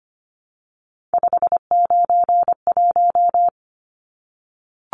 Morse code, heard as a keyed tone, about 25 wpm, 700 Hz. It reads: 591
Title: Numbers - 3-Digit 25wpm